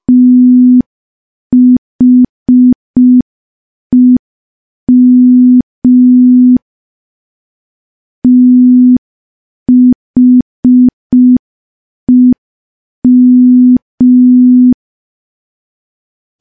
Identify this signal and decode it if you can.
Morse code, heard as a keyed tone, about 5 wpm, 250 Hz. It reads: THEM THEM
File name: morse / morse-training-05WPM